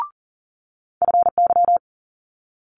Morse code, heard as a keyed tone, about 42 wpm, 700 Hz. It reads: FY
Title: Multi-letter Phonemes Quick Fire 40wpm